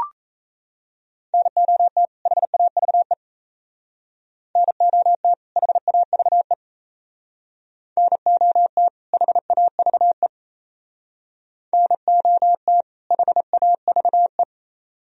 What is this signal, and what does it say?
Morse code, heard as a keyed tone, about 34 wpm, 700 Hz. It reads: NOT HAVE NOT HAVE NOT HAVE NOT HAVE
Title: Speed-Racing - Sets of 2 Words 28wpm